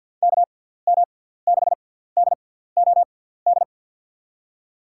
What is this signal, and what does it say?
Morse code, heard as a keyed tone, about 50 wpm, spaced as for 17 wpm, 700 Hz. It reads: KM6DOD
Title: USA Call Signs - 6 Characters - 6x Character Spacing - 50wpm